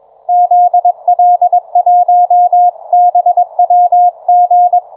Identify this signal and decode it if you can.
Morse code, heard as a keyed tone, about 23 wpm, 660 Hz. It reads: ZL1BWG